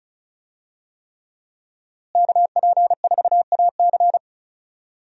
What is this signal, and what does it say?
Morse code, heard as a keyed tone, about 34 wpm, 700 Hz. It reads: KP4AC
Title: USA Call Signs - 5 Characters - Repeated 3x - 35wpm